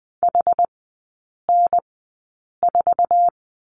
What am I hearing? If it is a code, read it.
Morse code, heard as a keyed tone, about 20 wpm, spaced as for 8 wpm, 700 Hz. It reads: HN4
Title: 067 Lesson 12e - Letter d - Three Characters 20wpm